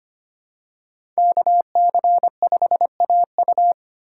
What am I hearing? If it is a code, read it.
Morse code, heard as a keyed tone, about 25 wpm, 700 Hz. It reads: KC5AU
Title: USA Call Signs - 5 Characters - 8x Character Spacing - 25wpm